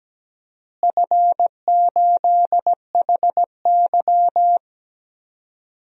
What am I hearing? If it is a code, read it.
Morse code, heard as a keyed tone, about 17 wpm, 700 Hz. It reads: F8HY
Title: DXCC-100 Call Signs - 4 Characters - Repeated 3x - 17wpm